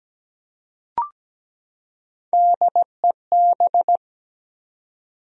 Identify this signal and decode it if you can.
Morse code, heard as a keyed tone, about 17 wpm, 700 Hz. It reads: DEB
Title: USA Common Names - Max 3 Letters - 17wpm